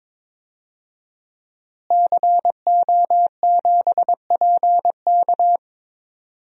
Morse code, heard as a keyed tone, about 22 wpm, 700 Hz. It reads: CO7PK
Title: DXCC-100 Call Signs - 5 Characters - Repeated 2x - 22wpm